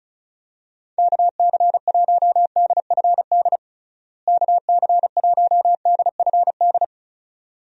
Morse code, heard as a keyed tone, about 36 wpm, 700 Hz. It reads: KC1DFD KC1DFD
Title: USA Call Signs - 6 Characters - Repeated 2x - 35wpm